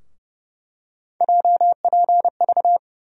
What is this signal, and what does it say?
Morse code, heard as a keyed tone, about 30 wpm, 700 Hz. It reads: JPV
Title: USA Call Sign Postfixes - 3 Letters 30wpm